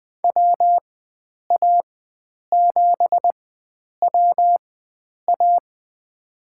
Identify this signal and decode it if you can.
Morse code, heard as a keyed tone, about 20 wpm, spaced as for 9 wpm, 700 Hz. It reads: WA7WA
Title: USA Call Signs - 5 Characters - 4x Character Spacing - 20wpm